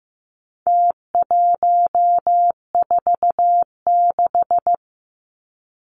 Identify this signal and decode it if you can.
Morse code, heard as a keyed tone, about 15 wpm, 700 Hz. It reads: T146
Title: Numbers - Contest Style - 4-Digit 15wpm